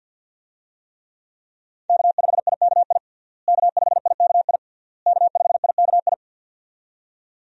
Morse code, heard as a keyed tone, about 49 wpm, 690 Hz. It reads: K5IKI K5IKI K5IKI
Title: USA Call Signs - 5 Characters - Repeated 3x - 50wpm